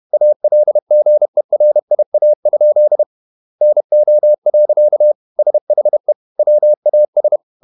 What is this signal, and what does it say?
Morse code, heard as a keyed tone, about 31 wpm, 600 Hz. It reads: ALGERIA? NO. SHE WAS
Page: Never 31wpm | CW med Gnister